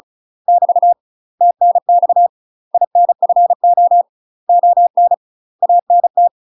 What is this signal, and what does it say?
Morse code, heard as a keyed tone, about 34 wpm, 700 Hz. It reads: <BT> TNX INFO ON ANT
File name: UK MASTER QSOs TRAINING / 35WPM / UK MASTER QSOs 0000